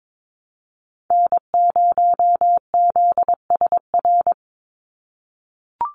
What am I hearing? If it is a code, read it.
Morse code, heard as a keyed tone, about 22 wpm, 700 Hz. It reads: N0ZSR
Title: USA Call Signs 22wpm